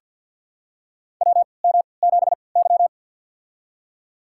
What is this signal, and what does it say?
Morse code, heard as a keyed tone, about 47 wpm, spaced as for 28 wpm, 700 Hz. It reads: WM7Y
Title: USA Call Signs - 4 Characters - 3x Character Spacing - 50wpm